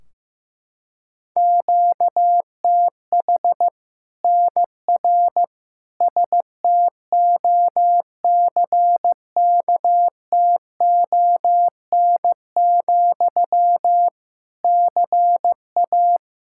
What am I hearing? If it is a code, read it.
Morse code, heard as a keyed tone, about 15 wpm, 700 Hz. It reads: QTH NR STOCKTON, CA
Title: Practice QSO - Exchange 1 - Beginning - 15wpm